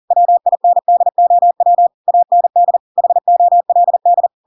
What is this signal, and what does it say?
Morse code, heard as a keyed tone, about 40 wpm, 700 Hz. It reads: WINDOW AND HOLD
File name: Pinocchio / Pinocchio ch6 40 40 ik1hge